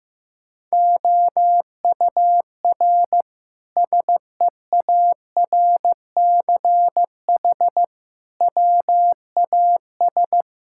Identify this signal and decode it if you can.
Morse code, heard as a keyed tone, about 15 wpm, 700 Hz. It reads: OUR SEARCH WAS
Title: Sentences from Top 1000 Words - Review Part 5 - 15wpm